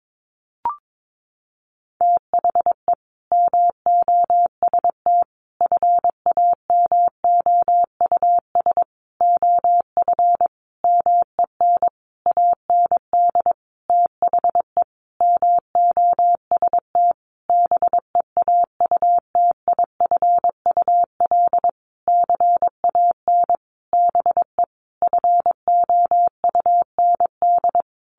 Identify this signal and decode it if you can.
Morse code, heard as a keyed tone, about 22 wpm, 700 Hz. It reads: THE MOST FAMOUS OF MEN AND THE MOST BEAUTIFUL CAN BE FOUND
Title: Sentences from Top 1500 Words 22wpm